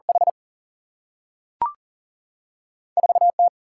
Morse code, heard as a keyed tone, about 42 wpm, 700 Hz. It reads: H 4 T
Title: Single-Letter Single-Number 40wpm